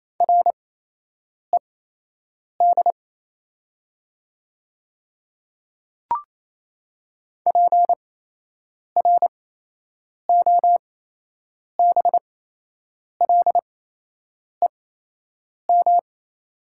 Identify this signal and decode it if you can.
Morse code, heard as a keyed tone, about 28 wpm, spaced as for 8 wpm, 700 Hz. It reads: RED PROBLEM
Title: 8x Character Spacing - Top 500 Words - 28wpm